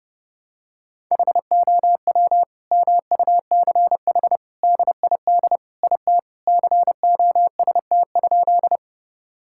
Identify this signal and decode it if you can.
Morse code, heard as a keyed tone, about 30 wpm, 700 Hz. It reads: HOW MUCH DID IT COST?
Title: Sentences from Top 1000 Words - Review Part 5 - 30wpm